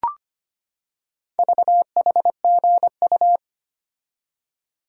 Morse code, heard as a keyed tone, about 25 wpm, 700 Hz. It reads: VHGU
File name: practice / 25wpm / Four Letter-Number Combinations - 25wpm